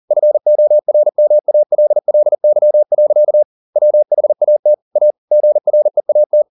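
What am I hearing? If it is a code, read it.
Morse code, heard as a keyed tone, about 40 wpm, 600 Hz. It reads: FORMALLY. WHAT A GREAT
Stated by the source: Morse code at 40 wpm, practice text, CW med Gnister.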